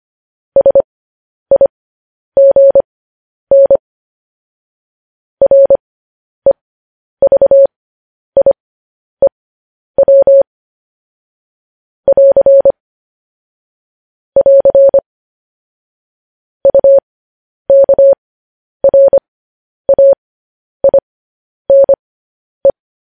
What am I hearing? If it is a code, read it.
Morse code, heard as a keyed tone, about 25 wpm, spaced as for 10 wpm, 550 Hz. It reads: SIGN REVIEW <AR> <AR> UKRAINE
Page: News Headlines in Morse Code at 10 WPM